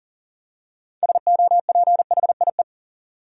Morse code, heard as a keyed tone, about 40 wpm, 700 Hz. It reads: SOPHIE
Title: USA Common Names - 40wpm